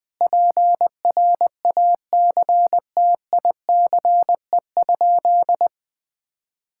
Morse code, heard as a keyed tone, about 20 wpm, 700 Hz. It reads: PRACTICE?
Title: Sentences from Top 1000 Words - Review Part 6 - 20wpm